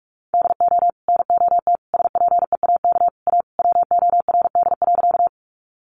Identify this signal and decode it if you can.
Morse code, heard as a keyed tone, about 44 wpm, 700 Hz. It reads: DO NOT SPEAK A WORD.
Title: Sentences from Top 1100 Words 45wpm